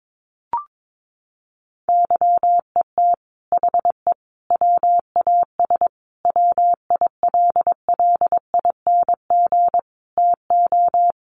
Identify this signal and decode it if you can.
Morse code, heard as a keyed tone, about 22 wpm, 700 Hz. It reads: YET HE WAS WILLING TO
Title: Sentences from Top 1100 Words 22wpm